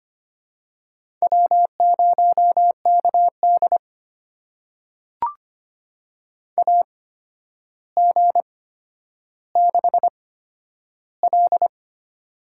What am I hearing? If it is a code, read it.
Morse code, heard as a keyed tone, about 25 wpm, spaced as for 11 wpm, 700 Hz. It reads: W0KD AG6L
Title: USA Call Signs - 8x Character Spacing - 25wpm